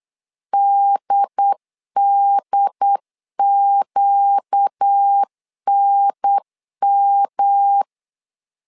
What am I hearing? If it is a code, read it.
Morse code, heard as a keyed tone, about 8 wpm, 800 Hz. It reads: DDQNM
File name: LEZIONI / 4 Q 8wpm 2732